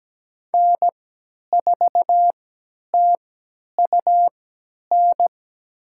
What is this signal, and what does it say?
Morse code, heard as a keyed tone, about 17 wpm, spaced as for 10 wpm, 700 Hz. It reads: N4TUN
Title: USA Call Signs - 5 Characters - 3x Character Spacing - 17wpm